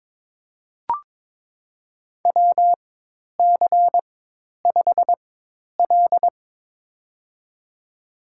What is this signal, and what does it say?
Morse code, heard as a keyed tone, about 22 wpm, spaced as for 10 wpm, 700 Hz. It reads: WC5L
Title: USA Call Signs - 4 Characters - 4x Character Spacing - 22wpm